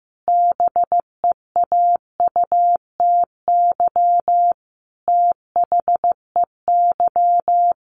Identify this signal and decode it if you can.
Morse code, heard as a keyed tone, about 15 wpm, 700 Hz. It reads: BEAUTY THEY
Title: Sentences from Top 1400 Words 15wpm